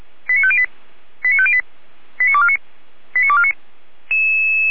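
Five-tone selective call (ZVEI), heard as quick runs of five tones, tones of about 80 ms. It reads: 89508 89508 89249 89249
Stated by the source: ZVEI five-tone call received on an SDR.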